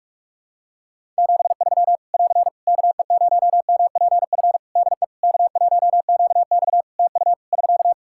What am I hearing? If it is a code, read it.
Morse code, heard as a keyed tone, about 48 wpm, 700 Hz. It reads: 73 <AR> KE0MPF DE K1QX TU <SK>
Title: Practice QSO - Exchange 4 - Ending - 45wpm